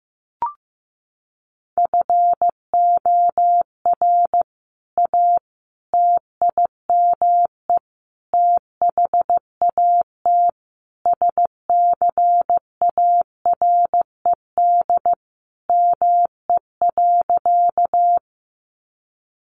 Morse code, heard as a keyed tone, about 15 wpm, 700 Hz. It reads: FOR A TIME THAT SCARED ME.
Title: Sentences from Top 700 Words 15wpm